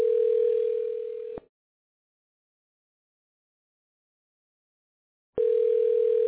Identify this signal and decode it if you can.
Ringback tone (North America), 440 + 480 Hz.